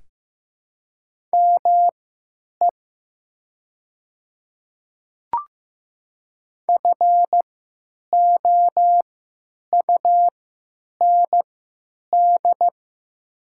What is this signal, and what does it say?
Morse code, heard as a keyed tone, about 15 wpm, spaced as for 9 wpm, 700 Hz. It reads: ME FOUND
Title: 3x Character Spacing - Top 300 Words - 15wpm